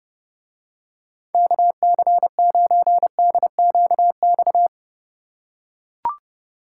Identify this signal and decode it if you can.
Morse code, heard as a keyed tone, about 30 wpm, 700 Hz. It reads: KC9DQX